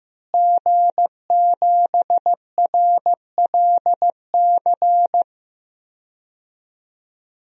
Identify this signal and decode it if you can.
Morse code, heard as a keyed tone, about 15 wpm, 700 Hz. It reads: G7RLC
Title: DXCC-100 Call Signs - 5 Characters - Repeated 2x - 15wpm